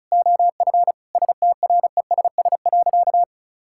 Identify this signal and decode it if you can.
Morse code, heard as a keyed tone, about 36 wpm, 700 Hz. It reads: OF STRESS.
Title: Sentences from Top 1000 Words - Review Part 6 - 35wpm